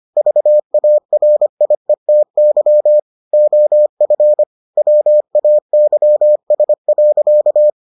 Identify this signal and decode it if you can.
Morse code, heard as a keyed tone, about 25 wpm, 600 Hz. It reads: VARIETY OF WAYS.